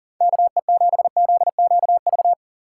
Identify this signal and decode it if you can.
Morse code, heard as a keyed tone, about 37 wpm, 700 Hz. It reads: KE7ZQV
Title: USA Call Signs - 6 Characters - 2x Character Spacing - 40wpm